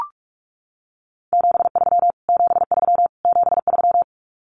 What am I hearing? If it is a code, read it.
Morse code, heard as a keyed tone, about 44 wpm, 700 Hz. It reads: 73 73 73
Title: Abbreviations Repeated 3x - 45wpm